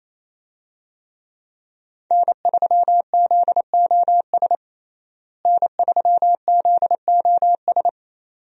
Morse code, heard as a keyed tone, about 28 wpm, 700 Hz. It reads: N3ZOS N3ZOS
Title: USA Call Signs - 5 Characters - Repeated 2x - 28wpm